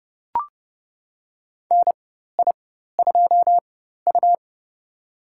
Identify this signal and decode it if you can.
Morse code, heard as a keyed tone, about 30 wpm, spaced as for 14 wpm, 700 Hz. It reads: NI2U
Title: USA Call Signs - 4 Characters - 4x Character Spacing - 30wpm